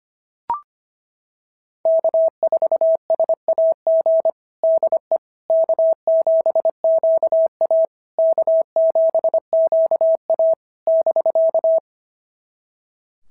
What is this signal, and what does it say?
Morse code, heard as a keyed tone, about 25 wpm, 650 Hz. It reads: K4SAG DE K7QA K7QA <BK>
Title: SOTA QSO Practice 25wpm